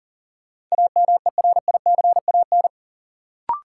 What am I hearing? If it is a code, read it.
Morse code, heard as a keyed tone, about 40 wpm, 700 Hz. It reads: AMERICAN
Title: Sets of 2 Words - New Word Drill - 40wpm